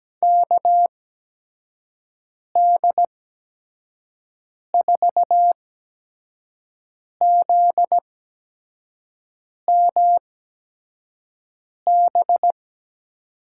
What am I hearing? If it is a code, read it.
Morse code, heard as a keyed tone, about 17 wpm, 700 Hz. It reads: K D 4 Z M B
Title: USA Call Signs - 6 Characters - 8x Character Spacing - 17wpm